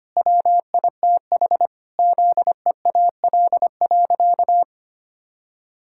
Morse code, heard as a keyed tone, about 25 wpm, 700 Hz. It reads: WITH ZEAL.
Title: Pangram Challenge v2 - 25wpm